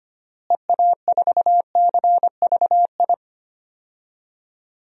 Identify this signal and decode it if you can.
Morse code, heard as a keyed tone, about 25 wpm, 700 Hz. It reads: EA4CVI
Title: DXCC-100 Call Signs - 6 Characters - Repeated 2x - 25wpm